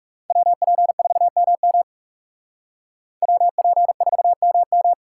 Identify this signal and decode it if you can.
Morse code, heard as a keyed tone, about 36 wpm, 700 Hz. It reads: WP4MM WP4MM
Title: Speed Racing - USA 2x2 Call Signs 30wpm